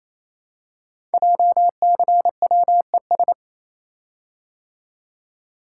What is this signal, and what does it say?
Morse code, heard as a keyed tone, about 29 wpm, 700 Hz. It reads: JCWES
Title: Code Groups - 5 Letters - 28wpm